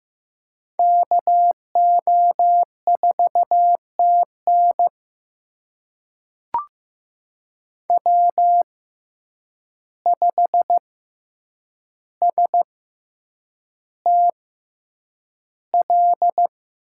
Morse code, heard as a keyed tone, about 15 wpm, spaced as for 8 wpm, 700 Hz. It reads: KO4TN W5STL
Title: USA Call Signs - 5 Characters - 6x Character Spacing - 15wpm